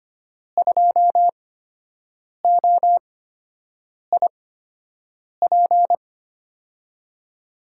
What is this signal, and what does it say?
Morse code, heard as a keyed tone, about 25 wpm, spaced as for 7 wpm, 700 Hz. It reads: 2OIP
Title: USA Call Signs - 5 Characters - 8x Character Spacing - 25wpm